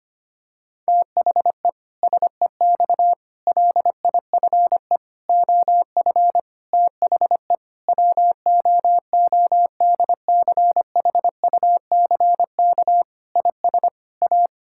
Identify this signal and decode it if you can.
Morse code, heard as a keyed tone, about 25 wpm, 700 Hz. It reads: THE SEX LIFE OF THE WOODCHUCK IS A
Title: Pangram Challenge v2 - 25wpm